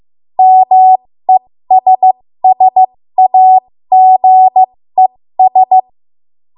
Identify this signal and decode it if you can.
Morse code, heard as a keyed tone, about 15 wpm, spaced as for 13 wpm, 750 Hz. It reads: MESSAGES
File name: Morse / 13WPM